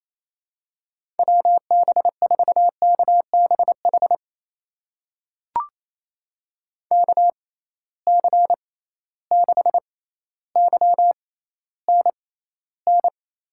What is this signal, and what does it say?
Morse code, heard as a keyed tone, about 28 wpm, spaced as for 14 wpm, 700 Hz. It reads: WB4KBH KC6YNN
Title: USA Call Signs - 6 Characters - 6x Character Spacing - 28wpm